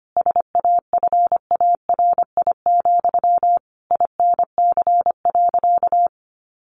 Morse code, heard as a keyed tone, about 25 wpm, 700 Hz. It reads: SAFARI, INC.
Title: A Sound Of Thunder 25wpm